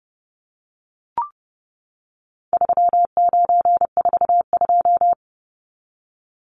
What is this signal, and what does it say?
Morse code, heard as a keyed tone, about 30 wpm, 700 Hz. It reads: 3942
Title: Numbers - 4-Digit 30wpm